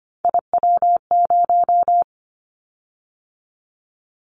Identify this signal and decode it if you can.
Morse code, heard as a keyed tone, about 25 wpm, 700 Hz. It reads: IW0
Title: Callsign Prefixes for Top 50 DXCC - 25wpm